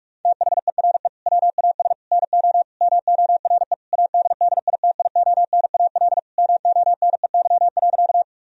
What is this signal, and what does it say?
Morse code, heard as a keyed tone, about 42 wpm, 700 Hz. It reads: THERE WAS NO MORE ADDITIONAL MONEY.
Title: Sentences from Top 1000 Words - Review Part 4 - 45wpm